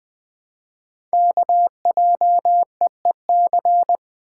Morse code, heard as a keyed tone, about 20 wpm, 700 Hz. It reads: KJEEC
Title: Code Groups - 5 Letters - 20wpm